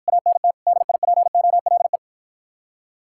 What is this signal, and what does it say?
Morse code, heard as a keyed tone, about 54 wpm, 700 Hz. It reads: ANT DIPOLE